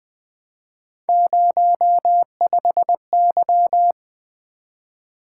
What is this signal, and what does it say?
Morse code, heard as a keyed tone, about 20 wpm, 700 Hz. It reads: 05Y
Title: Three Letter-Number Combinations - 20wpm